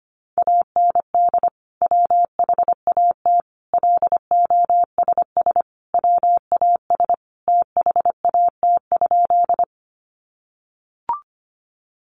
Morse code, heard as a keyed tone, about 25 wpm, 700 Hz. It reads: AND WHAT LOSS WAS THAT?
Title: Sentences from Top 1400 Words 25wpm